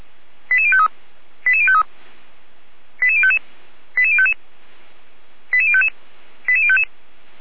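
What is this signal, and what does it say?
Five-tone selective call (ZVEI), heard as quick runs of five tones, tones of about 70 ms. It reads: 88053 88053 88055 88055 88055 88055
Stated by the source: ZVEI five-tone selective call received with an SDR.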